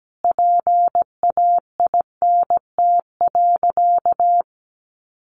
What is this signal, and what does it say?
Morse code, heard as a keyed tone, about 17 wpm, 700 Hz. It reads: PAINT.
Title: A Sound Of Thunder 17wpm